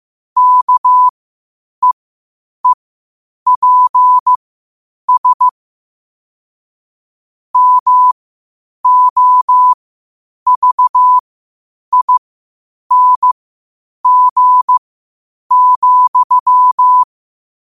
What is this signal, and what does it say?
Morse code, heard as a keyed tone, about 15 wpm, spaced as for 8 wpm, 1000 Hz. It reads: KEEPS MOVING,
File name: qotdfiles / 8wpm